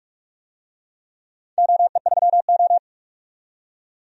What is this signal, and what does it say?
Morse code, heard as a keyed tone, about 46 wpm, 700 Hz. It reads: OE3O
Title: DXCC-100 Call Signs - 4 Characters - 45wpm